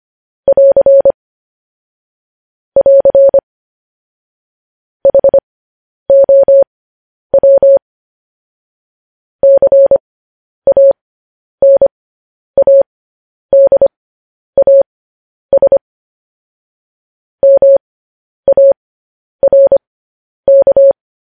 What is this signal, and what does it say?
Morse code, heard as a keyed tone, about 25 wpm, spaced as for 10 wpm, 550 Hz. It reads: <AR> <AR> HOW CANADAS MARK
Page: news-headlines-in-morse-code-at-10-wpm